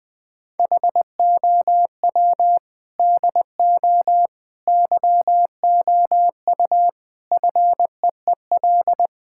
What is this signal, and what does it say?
Morse code, heard as a keyed tone, about 21 wpm, 700 Hz. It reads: HOW DO YOU FEEL
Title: Sentences with 5-Word Phrases - 20wpm